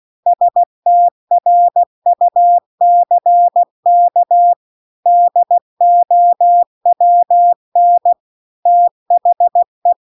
Morse code, heard as a keyed tone, about 16 wpm, 700 Hz. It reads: STRUCK DOWN THE